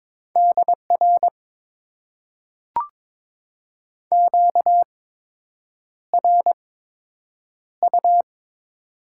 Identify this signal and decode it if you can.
Morse code, heard as a keyed tone, about 22 wpm, spaced as for 8 wpm, 700 Hz. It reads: DR QRU
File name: Top 90 QSO Elements - 8x Character Spacing - 22wpm